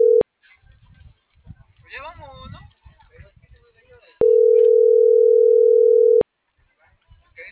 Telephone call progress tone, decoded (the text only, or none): ringback tone